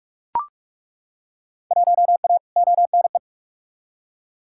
Morse code, heard as a keyed tone, about 41 wpm, 700 Hz. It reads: 1A ONE
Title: Field Day Exchanges Encore - 45wpm